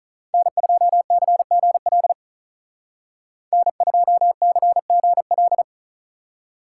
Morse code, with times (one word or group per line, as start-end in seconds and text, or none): 0.3-2.1 N2CGL
3.5-5.6 N2CGL